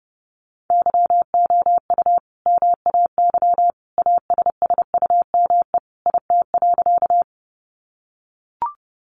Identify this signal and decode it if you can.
Morse code, heard as a keyed tone, about 30 wpm, 700 Hz. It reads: YOU MAY ASSUME IT.
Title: Sentences from Top 1500 Words 30wpm